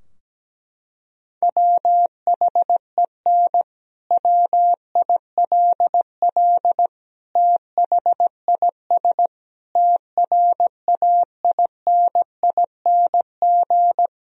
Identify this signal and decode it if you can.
Morse code, heard as a keyed tone, about 17 wpm, 700 Hz. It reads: WHEN WILL THIS TRAINING
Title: Sentences from Top 1000 Words - Review Part 4 - 17wpm